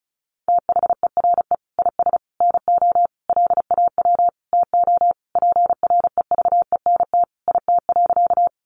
Morse code, heard as a keyed tone, about 34 wpm, 700 Hz. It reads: THERE IS NO LAW TO PREVENT IT.
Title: Sentences from Top 1100 Words 35wpm